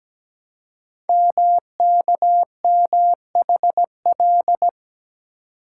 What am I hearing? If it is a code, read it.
Morse code, heard as a keyed tone, about 17 wpm, 700 Hz. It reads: MKMHL
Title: Code Groups - 5 Letters - 17wpm